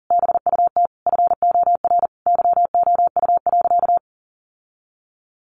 Morse code, heard as a keyed tone, about 40 wpm, 700 Hz. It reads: BUT FOR YOU.
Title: Sentences from Top 1300 Words 40wpm